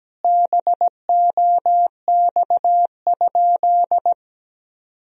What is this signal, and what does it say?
Morse code, heard as a keyed tone, about 17 wpm, 700 Hz. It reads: BOX?
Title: Sentences from Top 1000 Words - Review Part 2 - 17wpm